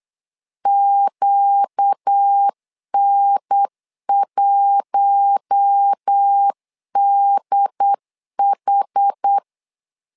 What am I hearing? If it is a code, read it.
Morse code, heard as a keyed tone, about 8 wpm, 800 Hz. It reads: QN1DH
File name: LEZIONI / 4 Q1 8wpm 3035